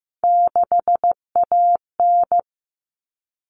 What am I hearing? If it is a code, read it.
Morse code, heard as a keyed tone, about 15 wpm, 700 Hz. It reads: 6AN
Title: All Cut Numbers - 3-Digit 15wpm